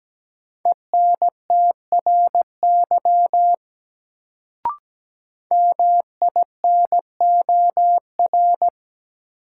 Morse code, heard as a keyed tone, about 17 wpm, 700 Hz. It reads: ENTRY MINOR
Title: Sentences from Top 1500 Words - New Word Drill - 17wpm